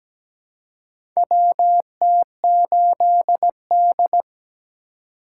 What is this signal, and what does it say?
Morse code, heard as a keyed tone, about 17 wpm, 700 Hz. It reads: WT8D
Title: USA Call Signs - 4 Characters - 2x Character Spacing - 17wpm